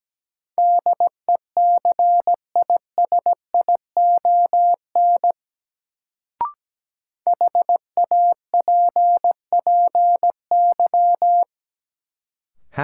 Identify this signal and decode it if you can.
Morse code, heard as a keyed tone, about 17 wpm, 700 Hz. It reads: DECISION HAPPY
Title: Sentences from Top 1000 Words - New Word Drill - 17wpm